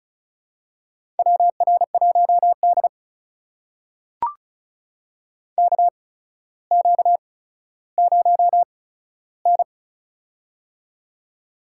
Morse code, heard as a keyed tone, about 36 wpm, spaced as for 15 wpm, 700 Hz. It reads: WR1D KQ0N
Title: USA Call Signs - 4 Characters - 8x Character Spacing - 35wpm